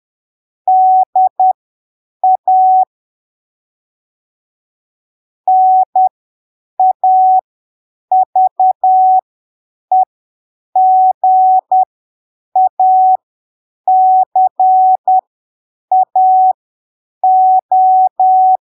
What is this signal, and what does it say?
Morse code, heard as a keyed tone, about 10 wpm, spaced as for 7 wpm, 750 Hz. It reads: DA NAVEGACAO
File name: arquivos / texto / texto 03 10ppm